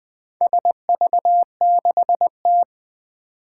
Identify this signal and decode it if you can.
Morse code, heard as a keyed tone, about 20 wpm, 700 Hz. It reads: SV6T
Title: Four Challenging Letter-Number Combinations 20wpm